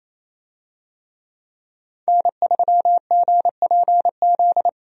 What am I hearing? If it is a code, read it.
Morse code, heard as a keyed tone, about 28 wpm, 700 Hz. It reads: N3GPZ
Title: USA Call Signs - 5 Characters - Repeated 2x - 28wpm